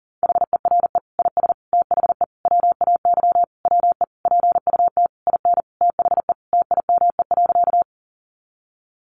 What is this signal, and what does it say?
Morse code, heard as a keyed tone, about 40 wpm, 700 Hz. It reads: HERE IS THE WAY WE PUT IN THE TIME.
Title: Sentences from Top 200 Words 40wpm